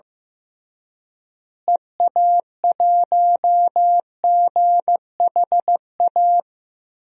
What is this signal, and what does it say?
Morse code, heard as a keyed tone, about 15 wpm, 700 Hz. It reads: EA1GHA
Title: DXCC-100 Call Signs - 6 Characters - Repeated 3x - 15wpm